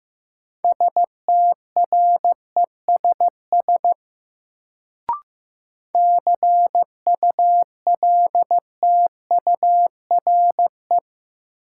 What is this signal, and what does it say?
Morse code, heard as a keyed tone, about 15 wpm, 700 Hz. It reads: STRESS CULTURE
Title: Sentences from Top 1000 Words - Review Part 6 - New Word Drill - 15wpm